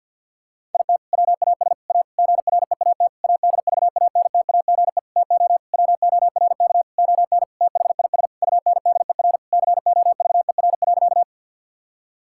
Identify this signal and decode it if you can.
Morse code, heard as a keyed tone, about 50 wpm, 700 Hz. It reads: IT WAS A GREAT ADVANTAGE TO WORK ON THIS UNDER COVER.
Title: Sentences from Top 1000 Words - Review Part 6 - 50wpm